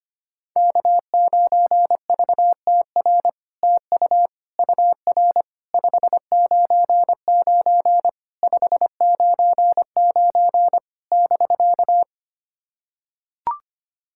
Morse code, heard as a keyed tone, about 25 wpm, 700 Hz. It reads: K9VTR TU UR 599 599 <BK>
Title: SOTA QSO Practice 25wpm